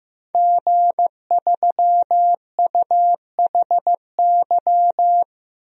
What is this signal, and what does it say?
Morse code, heard as a keyed tone, about 15 wpm, 700 Hz. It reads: G3UHY
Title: DXCC-100 Call Signs - 5 Characters - 15wpm